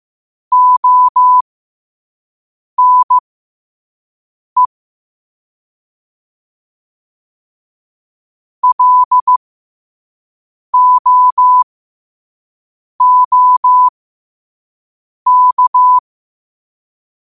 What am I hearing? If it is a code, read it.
Morse code, heard as a keyed tone, about 15 wpm, spaced as for 5 wpm, 1000 Hz. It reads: ONE LOOK
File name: qotdfiles / 5wpm